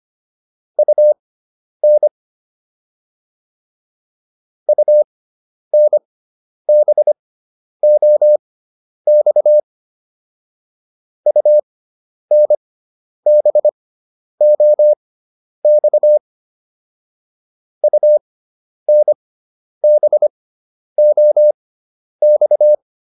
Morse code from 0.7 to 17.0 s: UN UNBOX UNBOX

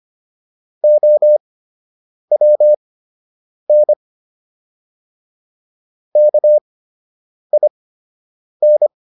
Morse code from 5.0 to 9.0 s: KIN

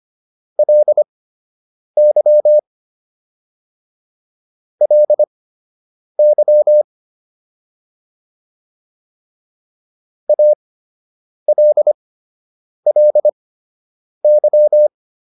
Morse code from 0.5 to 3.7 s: LY

